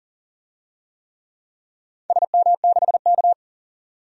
Morse code, heard as a keyed tone, about 40 wpm, 700 Hz. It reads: SM6K
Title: DXCC-100 Call Signs - 4 Characters - 40wpm